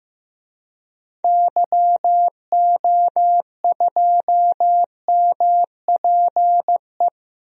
Morse code, heard as a keyed tone, about 15 wpm, 700 Hz. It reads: YO2MPE
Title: DXCC-100 Call Signs - 6 Characters - Repeated 2x - 15wpm